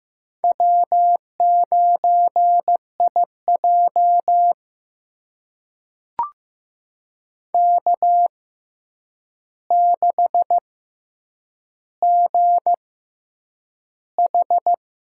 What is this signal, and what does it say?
Morse code, heard as a keyed tone, about 15 wpm, spaced as for 8 wpm, 700 Hz. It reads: W9IJ K6GH
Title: USA Call Signs - 6x Character Spacing - 15wpm